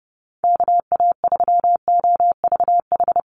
Morse code, heard as a keyed tone, about 30 wpm, 700 Hz. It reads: KA3OVH
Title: USA Call Signs Repeated 2x 30wpm